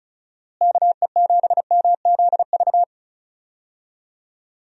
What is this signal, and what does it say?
Morse code, heard as a keyed tone, about 36 wpm, 700 Hz. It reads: KE7MZV
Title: USA Call Signs - 6 Characters - Repeated 2x - 35wpm